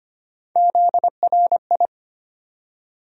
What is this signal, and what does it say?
Morse code, heard as a keyed tone, about 25 wpm, 700 Hz. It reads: ZRI
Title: USA Call Sign Postfixes - 3 Letters 25wpm